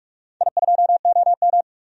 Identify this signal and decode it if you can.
Morse code, heard as a keyed tone, about 46 wpm, 700 Hz. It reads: I2OM
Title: DXCC-100 Call Signs - 4 Characters - 45wpm